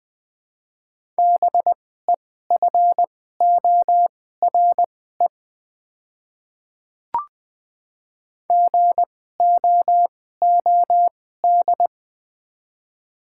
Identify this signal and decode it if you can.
Morse code, heard as a keyed tone, about 20 wpm, spaced as for 14 wpm, 700 Hz. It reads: BEFORE GOOD
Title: 2x Character Spacing - Top 200 Words - 20wpm